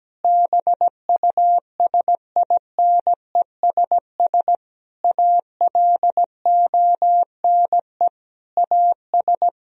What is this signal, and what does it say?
Morse code, heard as a keyed tone, about 17 wpm, 700 Hz. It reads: BUSINESS ALONE AS
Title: Sentences from Top 1000 Words - Review Part 2 - 17wpm